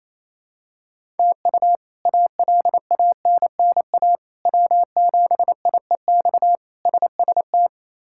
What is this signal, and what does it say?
Morse code, heard as a keyed tone, about 28 wpm, 700 Hz. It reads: TU ALANNA W7IEX SST
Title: K1USN SST QSO Practice - 28wpm